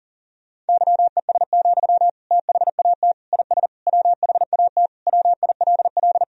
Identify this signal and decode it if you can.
Morse code, heard as a keyed tone, about 40 wpm, 700 Hz. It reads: YES, THAT IS WHAT WILL
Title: Sentences from Top 500 Words and Abbreviations 40wpm